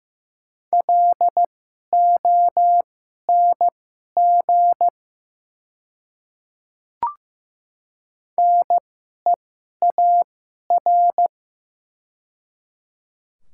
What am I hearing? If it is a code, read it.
Morse code, heard as a keyed tone, about 15 wpm, spaced as for 11 wpm, 700 Hz. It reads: LONG NEAR
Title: 2x Character Spacing - Top 200 Words - 15wpm